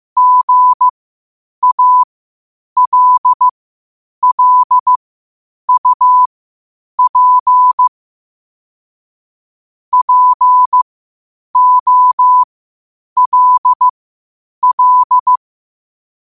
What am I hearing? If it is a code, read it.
Morse code, heard as a keyed tone, about 15 wpm, spaced as for 8 wpm, 1000 Hz. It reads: GALLUP POLL
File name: qotdfiles / 8wpm